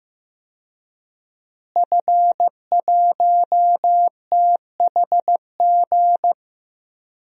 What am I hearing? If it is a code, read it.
Morse code, heard as a keyed tone, about 15 wpm, 700 Hz. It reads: F1THG
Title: DXCC-100 Call Signs - 5 Characters - Repeated 3x - 15wpm